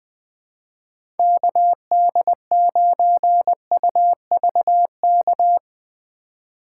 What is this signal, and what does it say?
Morse code, heard as a keyed tone, about 20 wpm, 700 Hz. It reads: KD9UVK
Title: 2023 POTA Hunter Call Signs - 20wpm